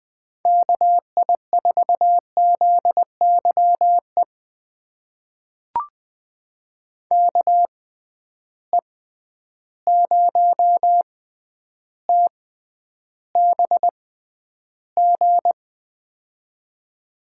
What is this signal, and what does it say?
Morse code, heard as a keyed tone, about 21 wpm, spaced as for 10 wpm, 700 Hz. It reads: KI4ZYE KE0TBG
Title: USA Call Signs - 6 Characters - 6x Character Spacing - 20wpm